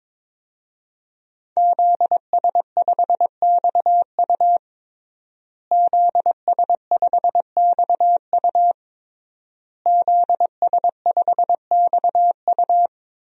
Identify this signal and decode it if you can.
Morse code, heard as a keyed tone, about 22 wpm, 700 Hz. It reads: ZS5XU ZS5XU ZS5XU
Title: DXCC-100 Call Signs - 5 Characters - Repeated 3x - 22wpm